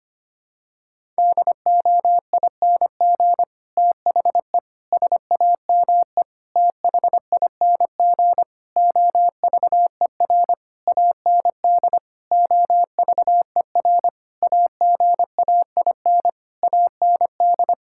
Morse code, heard as a keyed tone, about 25 wpm, 700 Hz. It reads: DOING THE SAME THING OVER AND OVER AGAIN AND
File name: practice / 25wpm / Sentences with 9-Word Phrases - 25wpm